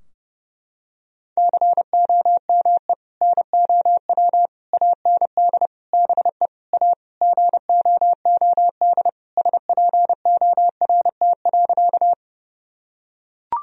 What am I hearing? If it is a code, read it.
Morse code, heard as a keyed tone, about 28 wpm, 700 Hz. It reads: COME NOW AND BE A GOOD SPORT.
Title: Sentences from Top 1000 Words - Review Part 4 - 30wpm